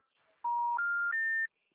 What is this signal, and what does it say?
Special information tone.